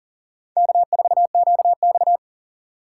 Morse code, heard as a keyed tone, about 51 wpm, 700 Hz. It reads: K4QX